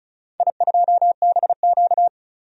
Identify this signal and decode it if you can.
Morse code, heard as a keyed tone, about 36 wpm, 700 Hz. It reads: I2BQ
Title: DXCC-100 Call Signs - 4 Characters - Repeated 3x - 35wpm